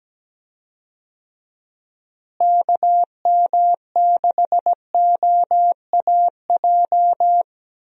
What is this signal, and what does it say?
Morse code, heard as a keyed tone, about 17 wpm, 700 Hz. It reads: KM6OAJ
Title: USA Call Signs - 6 Characters - Repeated 3x - 17wpm